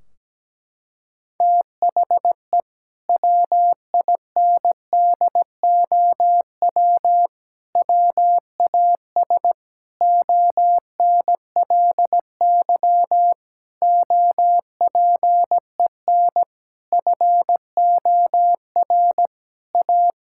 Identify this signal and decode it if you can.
Morse code, heard as a keyed tone, about 17 wpm, 700 Hz. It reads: THE WINDOW WAS ONLY OPEN FOR A
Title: Sentences from Top 1600 Words 17wpm